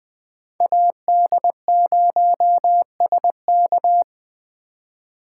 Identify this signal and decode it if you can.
Morse code, heard as a keyed tone, about 20 wpm, 700 Hz. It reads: AD0SK
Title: USA Call Signs - 5 Characters - 2x Character Spacing - 20wpm